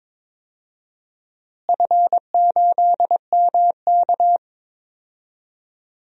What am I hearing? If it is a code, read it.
Morse code, heard as a keyed tone, about 22 wpm, 700 Hz. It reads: F8MK
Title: DXCC-100 Call Signs - 4 Characters - Repeated 2x - 22wpm